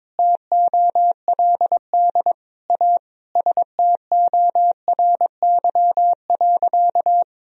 Morse code, heard as a keyed tone, about 21 wpm, 700 Hz. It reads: TOLD A STORY.
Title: Sentences from Top 1000 Words - Review Part 1 - 22wpm